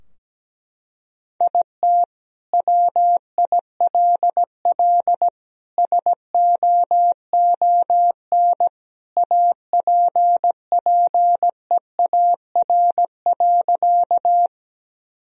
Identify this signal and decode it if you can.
Morse code, heard as a keyed tone, about 17 wpm, 700 Hz. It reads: IT WILL SOON APPEAR.
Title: Sentences from Top 1000 Words - Review Part 2 - 17wpm